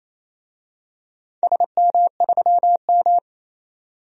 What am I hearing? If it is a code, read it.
Morse code, heard as a keyed tone, about 27 wpm, 690 Hz. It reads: SM3M
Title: DXCC-100 Call Signs - 4 Characters - 28wpm